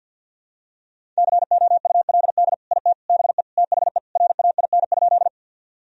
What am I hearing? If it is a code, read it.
Morse code, heard as a keyed tone, about 52 wpm, 700 Hz. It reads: COULD IT BE THE RAIN?
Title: Sentences from Top 1000 Words - Review Part 1 - 50wpm